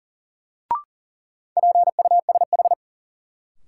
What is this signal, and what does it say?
Morse code, heard as a keyed tone, about 37 wpm, 700 Hz. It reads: PUSH
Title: Sentences from Top 1300 Words - New Word Drill - 40wpm